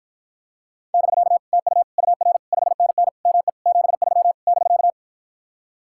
Morse code, heard as a keyed tone, about 54 wpm, 700 Hz. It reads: <BK> TU UR 5NN ME 73 <BK>